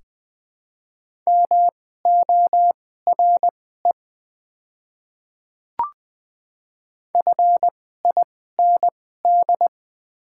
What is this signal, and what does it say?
Morse code, heard as a keyed tone, about 20 wpm, spaced as for 15 wpm, 700 Hz. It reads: MORE FIND
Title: 2x Character Spacing - Top 300 Words - 20wpm